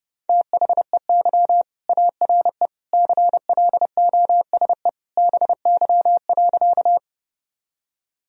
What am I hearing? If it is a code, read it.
Morse code, heard as a keyed tone, about 30 wpm, 700 Hz. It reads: THEY ARE CLOSE BY.
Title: Sentences from Top 1000 Words - Review Part 1 - 30wpm